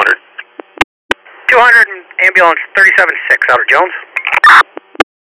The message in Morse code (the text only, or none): none